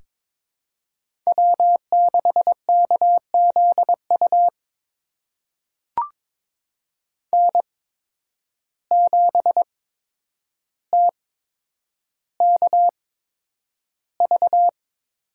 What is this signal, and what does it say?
Morse code, heard as a keyed tone, about 22 wpm, spaced as for 9 wpm, 700 Hz. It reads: W6KZU N7TKV
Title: USA Call Signs - 5 Characters - 8x Character Spacing - 22wpm